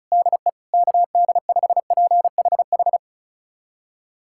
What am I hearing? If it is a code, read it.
Morse code, heard as a keyed tone, about 36 wpm, 690 Hz. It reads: DE KD5PHH
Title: USA Callsign DE Callsign 35wpm